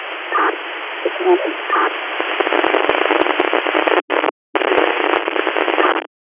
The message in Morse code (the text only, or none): none